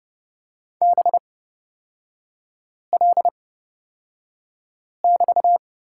Morse code, heard as a keyed tone, about 29 wpm, spaced as for 5 wpm, 700 Hz. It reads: BL<BT>
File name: Lesson 36e - BT - Three Characters 30wpm